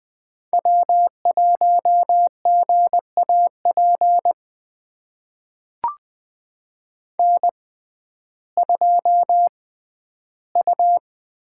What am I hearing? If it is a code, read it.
Morse code, heard as a keyed tone, about 20 wpm, spaced as for 12 wpm, 700 Hz. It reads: W1GAP N2U
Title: USA Call Signs - 5 Characters - 6x Character Spacing - 20wpm